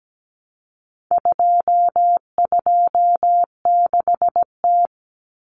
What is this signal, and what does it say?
Morse code, heard as a keyed tone, about 17 wpm, 700 Hz. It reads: 226T
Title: Common Cut Numbers - 4-Digit 17wpm